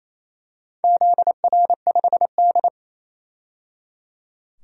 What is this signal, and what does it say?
Morse code, heard as a keyed tone, about 28 wpm, 700 Hz. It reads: ZR5D
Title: DXCC-100 Call Signs - 4 Characters - 28wpm